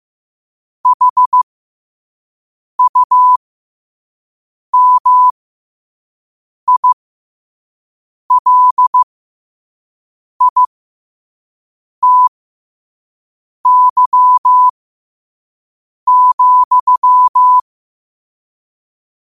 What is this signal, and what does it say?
Morse code, heard as a keyed tone, about 15 wpm, spaced as for 5 wpm, 1000 Hz. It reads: HUMILITY,
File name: qotdfiles / 5wpm